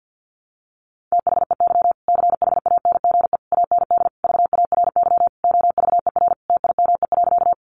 Morse code, heard as a keyed tone, about 50 wpm, 700 Hz. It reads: THEY CHANGE AND VARY OVER TIME.
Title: Sentences from Top 1200 Words 50wpm